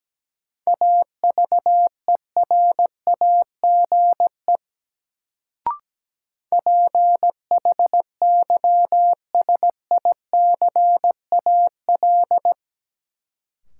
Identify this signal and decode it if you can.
Morse code, heard as a keyed tone, about 17 wpm, 700 Hz. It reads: AVERAGE PHYSICAL
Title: Sentences from Top 1400 Words - New Word Drill Alt - 17wpm